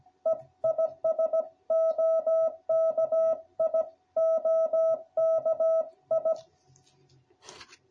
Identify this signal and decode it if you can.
Morse code, heard as a keyed tone, about 16 wpm, 660 Hz. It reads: E IS OKI OKI